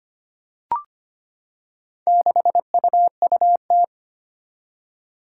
Morse code, heard as a keyed tone, about 25 wpm, 700 Hz. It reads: 6UUT
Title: Four Challenging Letter-Number Combinations 25wpm